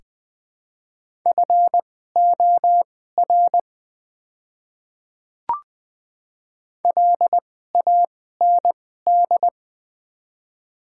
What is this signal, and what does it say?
Morse code, heard as a keyed tone, about 20 wpm, spaced as for 15 wpm, 700 Hz. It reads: FOR LAND
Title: 2x Character Spacing - Top 200 Words - 20wpm